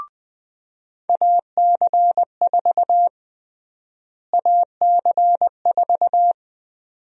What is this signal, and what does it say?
Morse code, heard as a keyed tone, about 20 wpm, 700 Hz. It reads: AC4 AC4
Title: USA Call Sign Prefixes - Repeated 2x - 20wpm